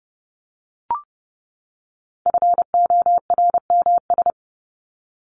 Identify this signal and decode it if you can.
Morse code, heard as a keyed tone, about 30 wpm, 700 Hz. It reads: FORMS
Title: ES:S Suffix - Top 100 Words with Base Words 30wpm